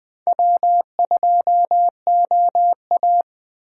Morse code, heard as a keyed tone, about 20 wpm, 700 Hz. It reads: W2OA